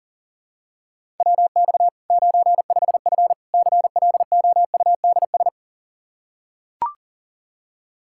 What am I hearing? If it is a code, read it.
Morse code, heard as a keyed tone, about 40 wpm, 700 Hz. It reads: WX 95F CLOUDS
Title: K1USN SST QSO Practice 40wpm